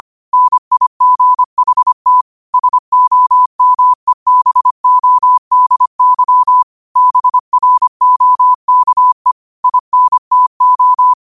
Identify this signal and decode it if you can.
Morse code, heard as a keyed tone, about 25 wpm, 1000 Hz. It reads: NIGHT SOMEBODY BROKE INTO